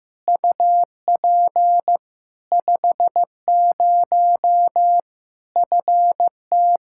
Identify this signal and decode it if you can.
Morse code, heard as a keyed tone, about 15 wpm, 700 Hz. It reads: UP 50 FT
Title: Practice QSO - Exchange 2 - Info - 15wpm